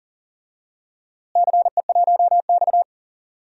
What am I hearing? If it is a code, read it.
Morse code, heard as a keyed tone, about 40 wpm, 700 Hz. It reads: CE1X
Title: DXCC-100 Call Signs - 4 Characters - Repeated 2x - 40wpm